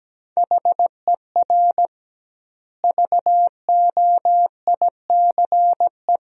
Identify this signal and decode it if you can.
Morse code, heard as a keyed tone, about 17 wpm, 700 Hz. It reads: HER VOICE
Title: 2x Word-Spacing - Sentences from Top 500 Words - 17wpm